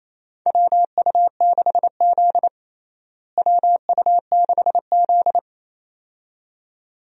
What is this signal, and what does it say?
Morse code, heard as a keyed tone, about 28 wpm, 700 Hz. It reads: WU6Z WU6Z
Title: USA Call Signs - 4 Characters - Repeated 2x - 28wpm